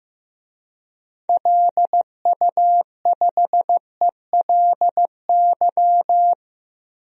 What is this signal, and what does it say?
Morse code, heard as a keyed tone, about 15 wpm, 700 Hz. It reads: LU5ELY
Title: DXCC-100 Call Signs - 6 Characters - Repeated 2x - 15wpm